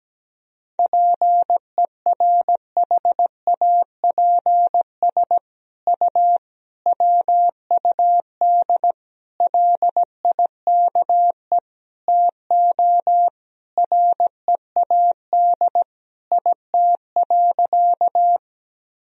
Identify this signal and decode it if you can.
Morse code, heard as a keyed tone, about 17 wpm, 700 Hz. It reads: PERHAPS U WUD LIKE TO READ IT.
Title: Sentences from Top 500 Words and Abbreviations 17wpm